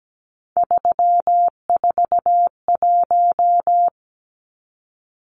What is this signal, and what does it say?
Morse code, heard as a keyed tone, about 17 wpm, 700 Hz. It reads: 341
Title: Numbers - 3-Digit 17wpm